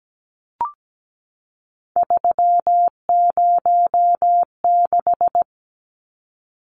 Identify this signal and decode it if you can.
Morse code, heard as a keyed tone, about 17 wpm, 700 Hz. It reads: 306
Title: Numbers - 3-Digit 17wpm